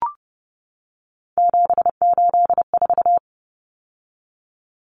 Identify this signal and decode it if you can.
Morse code, heard as a keyed tone, about 30 wpm, 700 Hz. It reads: 784
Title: Numbers - 3-Digit 30wpm